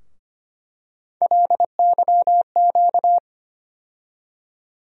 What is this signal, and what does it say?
Morse code, heard as a keyed tone, about 25 wpm, 700 Hz. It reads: LYQ
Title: USA Call Sign Postfixes - 3 Letters 25wpm